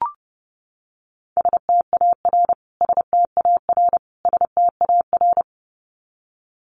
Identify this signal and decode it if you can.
Morse code, heard as a keyed tone, about 30 wpm, 700 Hz. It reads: STAR STAR STAR
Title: Top 500 Words Repeated 30wpm